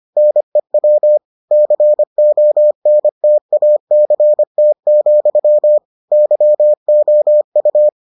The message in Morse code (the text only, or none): NEW CONTACT, YOU